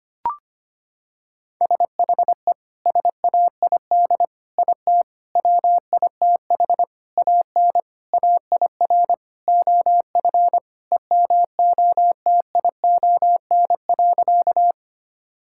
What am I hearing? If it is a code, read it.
Morse code, heard as a keyed tone, about 25 wpm, 700 Hz. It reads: SHE SAID IT WITH AN AIR OF EMOTION.
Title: Sentences from Top 1600 Words 25wpm